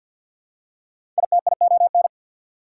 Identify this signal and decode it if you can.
Morse code, heard as a keyed tone, about 48 wpm, 690 Hz. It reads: ITION
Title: Prefixes and Suffixes 50wpm